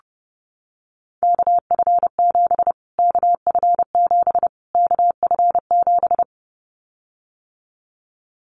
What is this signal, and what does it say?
Morse code, heard as a keyed tone, about 28 wpm, 700 Hz. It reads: KF7 KF7 KF7
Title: Callsign Prefixes for Top 25 DXCC - Repeated 3x - 30wpm